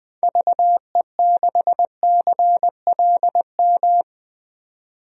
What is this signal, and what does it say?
Morse code, heard as a keyed tone, about 20 wpm, 700 Hz. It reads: VE6CLM
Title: DXCC-100 Call Signs - 6 Characters - 20wpm